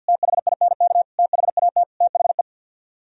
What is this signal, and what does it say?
Morse code, heard as a keyed tone, about 64 wpm, 700 Hz. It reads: THINK THAT THE